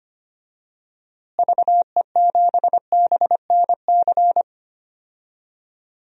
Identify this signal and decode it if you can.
Morse code, heard as a keyed tone, about 25 wpm, 700 Hz. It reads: VE7BNC
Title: DXCC-100 Call Signs - 6 Characters - Repeated 2x - 25wpm